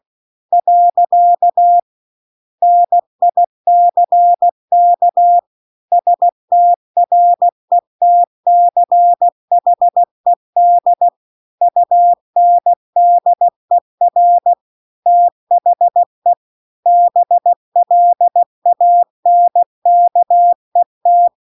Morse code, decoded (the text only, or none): . NICK STRETCHED UNDER THE BLANKET